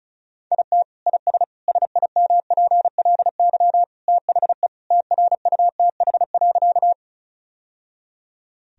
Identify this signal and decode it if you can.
Morse code, heard as a keyed tone, about 34 wpm, 700 Hz. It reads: IT IS SIMPLY THE TRUTH.
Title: Sentences from Top 1000 Words - Review Part 4 - 35wpm